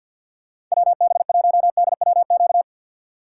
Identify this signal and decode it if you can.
Morse code, heard as a keyed tone, about 52 wpm, 700 Hz. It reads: WB1DWQ
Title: USA Call Signs - 6 Characters - 3x Character Spacing - 50wpm